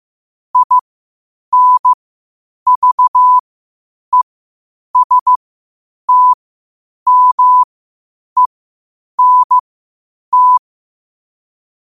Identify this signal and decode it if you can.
Morse code, heard as a keyed tone, about 15 wpm, spaced as for 8 wpm, 1000 Hz. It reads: INVESTMENT
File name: qotdfiles / 8wpm